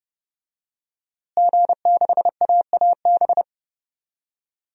Morse code, heard as a keyed tone, about 30 wpm, 700 Hz. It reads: G6AAB
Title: DXCC-100 Call Signs - 5 Characters - Repeated 2x - 30wpm